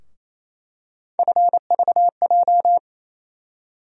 Morse code, heard as a keyed tone, about 28 wpm, 700 Hz. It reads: FVJ
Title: USA Call Sign Postfixes - 3 Letters 28wpm